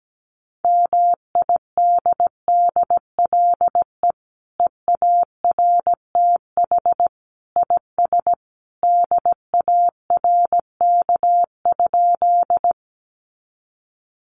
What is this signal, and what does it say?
Morse code, heard as a keyed tone, about 17 wpm, 700 Hz. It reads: MIDDLE EARTH IS DARK?
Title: Sentences from Top 900 Words 17wpm